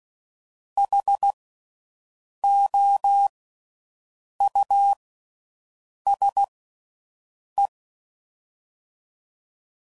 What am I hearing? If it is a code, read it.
Morse code, heard as a keyed tone, about 16 wpm, 800 Hz. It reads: H O U S E
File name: Suffixes / General / Houses